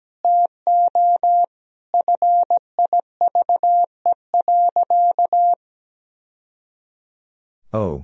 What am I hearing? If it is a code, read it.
Morse code, heard as a keyed tone, about 17 wpm, 700 Hz. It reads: TO FIVE.
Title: Sentences from Top 1000 Words - Review Part 2 - 17wpm